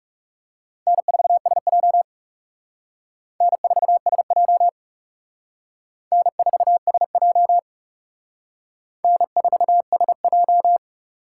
Morse code, heard as a keyed tone, about 38 wpm, 700 Hz. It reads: N4SJ N4SJ N4SJ N4SJ